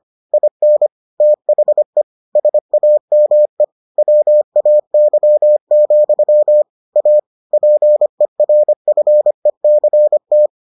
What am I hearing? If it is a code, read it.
Morse code, heard as a keyed tone, about 25 wpm, 600 Hz. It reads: IN THE SAME WAY, A PERFECT